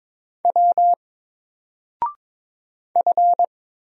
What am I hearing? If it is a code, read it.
Morse code, heard as a keyed tone, about 22 wpm, 700 Hz. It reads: W F
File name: Single-Letter-Number-22wpm